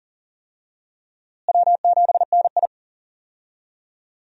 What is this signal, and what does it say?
Morse code, heard as a keyed tone, about 40 wpm, 700 Hz. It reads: W7NI
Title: USA Call Signs - 4 Characters - Repeated 2x - 40wpm